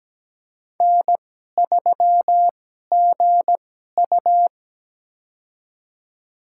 Morse code, heard as a keyed tone, about 17 wpm, spaced as for 12 wpm, 700 Hz. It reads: N3GU
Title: USA Call Signs - 4 Characters - 2x Character Spacing - 17wpm